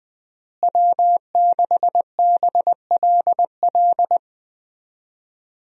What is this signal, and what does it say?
Morse code, heard as a keyed tone, about 21 wpm, 700 Hz. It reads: W6BLL
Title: USA Call Signs - 5 Characters - Repeated 2x - 20wpm